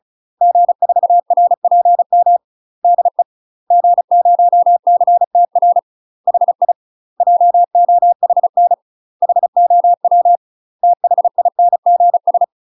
Morse code, heard as a keyed tone, about 35 wpm, 700 Hz. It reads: G4RPM DE G0CTR HI JOHN HOW THINGS
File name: UK MASTER QSOs TRAINING / 35WPM / UK MASTER QSOs 0018